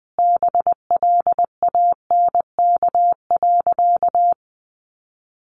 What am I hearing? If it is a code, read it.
Morse code, heard as a keyed tone, about 20 wpm, 700 Hz. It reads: BLANK.
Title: Sentences from Top 1500 Words 20wpm